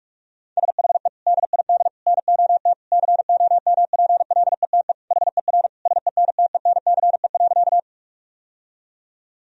Morse code, heard as a keyed tone, about 47 wpm, 700 Hz. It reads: SHE DID NOT COMPLETE HER SENTENCE.
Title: Sentences from Top 1000 Words - Review Part 2 - 45wpm